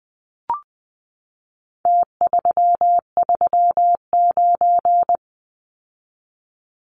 Morse code, heard as a keyed tone, about 20 wpm, 700 Hz. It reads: T339
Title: Numbers - Contest Style - 4-Digit 20wpm